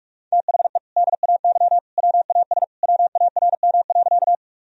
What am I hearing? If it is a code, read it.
Morse code, heard as a keyed tone, about 45 wpm, 700 Hz. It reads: THE DAY WAS WARM.
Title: Sentences from Top 1000 Words - Review Part 2 - 45wpm